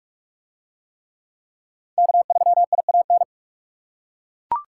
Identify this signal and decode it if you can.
Morse code, heard as a keyed tone, about 40 wpm, 700 Hz. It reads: K3IAN